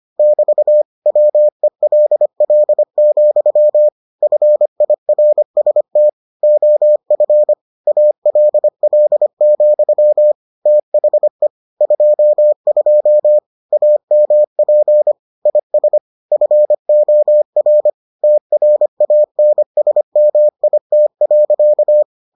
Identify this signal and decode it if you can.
Morse code, heard as a keyed tone, about 25 wpm, 600 Hz. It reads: <BT> WELL, FIRST OF ALL, THE 22 AMP IS FOR TRANSMIT.